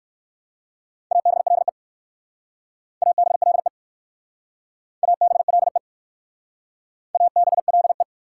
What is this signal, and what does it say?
Morse code, heard as a keyed tone, about 54 wpm, 690 Hz. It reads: ABLE ABLE ABLE ABLE
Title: Speed-Racing Prefixes and Suffixes 45wpm